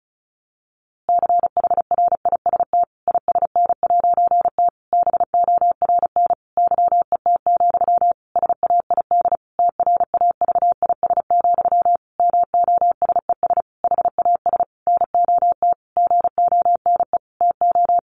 Morse code, heard as a keyed tone, about 34 wpm, 700 Hz. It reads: CHRIST ISN'T BORN YET, SAID TRAVIS, MOSES HAS NOT GONE TO
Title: A Sound Of Thunder 35wpm